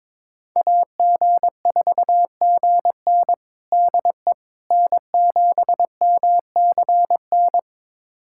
Morse code, heard as a keyed tone, about 21 wpm, 700 Hz. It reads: AG4GN DE N7MCN
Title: USA Callsign DE Callsign - 5 Characters - 22wpm